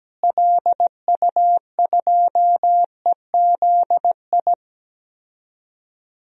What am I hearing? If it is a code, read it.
Morse code, heard as a keyed tone, about 17 wpm, 700 Hz. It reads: LU2EZI
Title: DXCC-100 Call Signs - 6 Characters - Repeated 2x - 17wpm